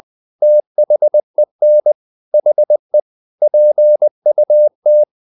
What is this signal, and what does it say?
Morse code, heard as a keyed tone, about 20 wpm, 600 Hz. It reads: THEN HE PUT
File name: the old man and the sea part 2 20wpm